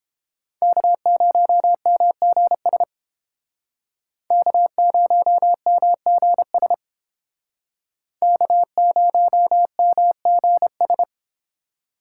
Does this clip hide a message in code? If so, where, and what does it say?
Yes: Morse code from 0.6 to 11.0 s, heard as a keyed tone, about 29 wpm, 700 Hz. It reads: K0MGS K0MGS K0MGS